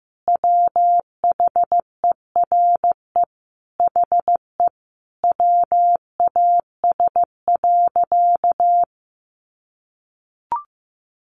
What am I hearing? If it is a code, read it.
Morse code, heard as a keyed tone, about 15 wpm, 700 Hz. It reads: WHERE HE WAS.
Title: Sentences from Top 600 Words 15wpm